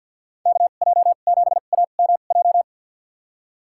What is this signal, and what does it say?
Morse code, heard as a keyed tone, about 48 wpm, spaced as for 35 wpm, 700 Hz. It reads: KJ7AMJ
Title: USA Call Signs - 2x Character Spacing - 50wpm